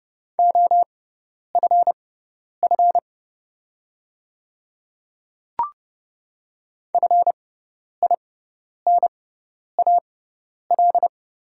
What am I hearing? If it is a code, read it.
Morse code, heard as a keyed tone, about 30 wpm, spaced as for 10 wpm, 690 Hz. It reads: OFF FINAL
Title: 6x Character Spacing - Top 500 Words - 30wpm